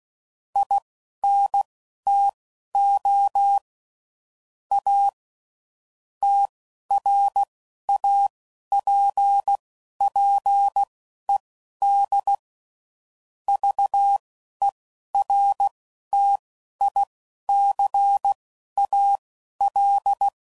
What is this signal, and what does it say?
Morse code, heard as a keyed tone, about 16 wpm, spaced as for 11 wpm, 800 Hz. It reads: INTO A TRAPPED VERTICAL
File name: CW Code Groups @10WPM / QSOs / QSO kc7sac